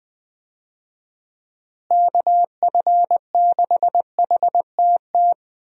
Morse code, heard as a keyed tone, about 20 wpm, 700 Hz. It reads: KF6HTT
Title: USA Call Signs - 6 Characters - Repeated 3x - 20wpm